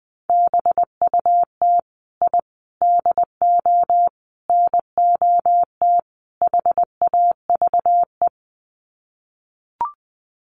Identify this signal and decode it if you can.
Morse code, heard as a keyed tone, about 20 wpm, 700 Hz. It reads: BUT I DO NOT HAVE
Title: Sets of 5 Words Encore 20wpm